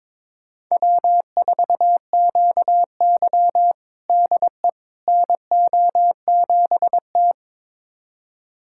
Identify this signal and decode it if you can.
Morse code, heard as a keyed tone, about 22 wpm, 700 Hz. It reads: W4QY DE NO7T
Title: USA Callsign DE Callsign - 4 Characters - 22wpm